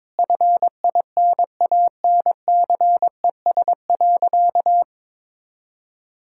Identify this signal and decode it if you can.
Morse code, heard as a keyed tone, about 21 wpm, 700 Hz. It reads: FINANCES.
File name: Difficult Sentences with Hints - 22wpm